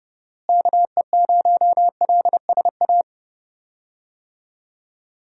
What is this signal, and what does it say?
Morse code, heard as a keyed tone, about 30 wpm, 700 Hz. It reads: KE0LSA
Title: USA Call Signs - 6 Characters - Repeated 2x - 30wpm